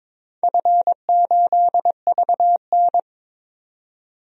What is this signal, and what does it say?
Morse code, heard as a keyed tone, about 23 wpm, 700 Hz. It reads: F8VN
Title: DXCC-100 Call Signs - 4 Characters - 22wpm